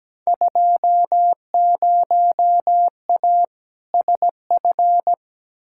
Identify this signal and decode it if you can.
Morse code, heard as a keyed tone, about 17 wpm, 700 Hz. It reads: 20A SF
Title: Field Day Exchanges Encore - 17wpm